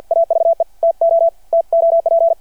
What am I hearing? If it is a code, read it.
Morse code, heard as a keyed tone, about 53 wpm, 660 Hz. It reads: AVE TO TOP
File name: archived / CRSnet